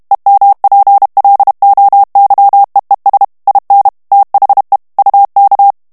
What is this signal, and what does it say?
Morse code, heard as a keyed tone, about 32 wpm, 800 Hz. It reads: EMPLOYEES IN THE UK
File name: morse mp3 / ANGLAIS texte clair 1 vitesse 30 mots minute